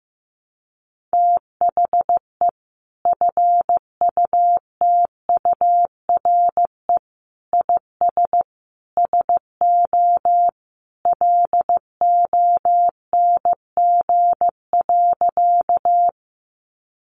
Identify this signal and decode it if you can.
Morse code, heard as a keyed tone, about 15 wpm, 700 Hz. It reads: THE FUTURE IS SO LONG.
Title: Sentences from Top 700 Words 15wpm